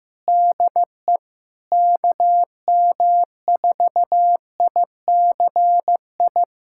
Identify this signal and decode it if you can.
Morse code, heard as a keyed tone, about 15 wpm, 700 Hz. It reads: DE KM4ICI
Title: USA Callsign DE Callsign 15wpm